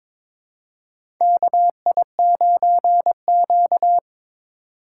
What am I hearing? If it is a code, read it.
Morse code, heard as a keyed tone, about 22 wpm, 700 Hz. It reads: KI9Q
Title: USA Call Signs - 4 Characters - 2x Character Spacing - 22wpm